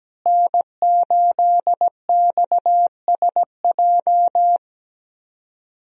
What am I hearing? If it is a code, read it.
Morse code, heard as a keyed tone, about 17 wpm, 700 Hz. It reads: N8XSJ
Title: USA Call Signs - 5 Characters - Repeated 3x - 17wpm